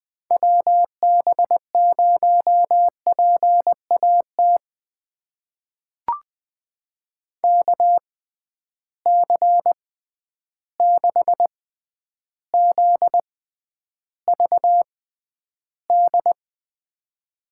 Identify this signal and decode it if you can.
Morse code, heard as a keyed tone, about 21 wpm, spaced as for 10 wpm, 700 Hz. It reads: WB0PAT KC6ZVD
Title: USA Call Signs - 6 Characters - 6x Character Spacing - 20wpm